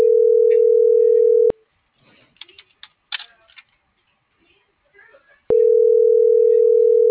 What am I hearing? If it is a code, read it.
Ringback tone (North America), 440 + 480 Hz.